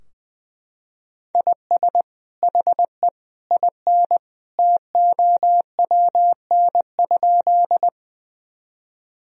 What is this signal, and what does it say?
Morse code, heard as a keyed tone, about 20 wpm, 700 Hz. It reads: IS HE IN TOWN?
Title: Sentences from Top 1000 Words - Review Part 2 - 20wpm